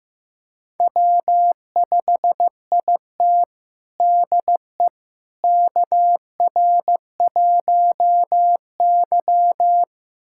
Morse code, heard as a keyed tone, about 15 wpm, 700 Hz. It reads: W5IT DE KR1Y
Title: USA Callsign DE Callsign - 4 Characters - 15wpm